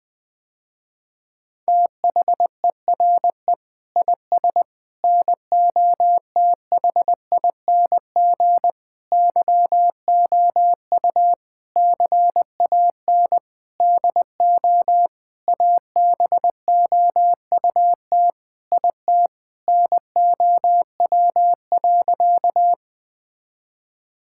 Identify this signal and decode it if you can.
Morse code, heard as a keyed tone, about 20 wpm, 700 Hz. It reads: THERE IS NOTHING YOU CAN DO ABOUT IT NOW.
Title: Sentences with 7-Word Phrases - Spoken Beginning - 20wpm